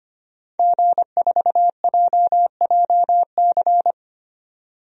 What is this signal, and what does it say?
Morse code, heard as a keyed tone, about 25 wpm, 700 Hz. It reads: G4JJC
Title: DXCC-100 Call Signs - 5 Characters - Repeated 2x - 25wpm